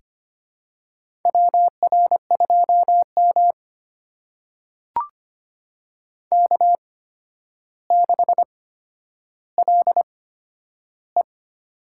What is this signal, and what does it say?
Morse code, heard as a keyed tone, about 25 wpm, spaced as for 11 wpm, 700 Hz. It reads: WR2M K6LE
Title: USA Call Signs - 4 Characters - 8x Character Spacing - 25wpm